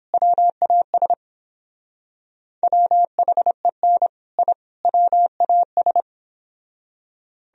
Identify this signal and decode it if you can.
Morse code, heard as a keyed tone, about 29 wpm, 700 Hz. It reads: WAS WHEN I WAS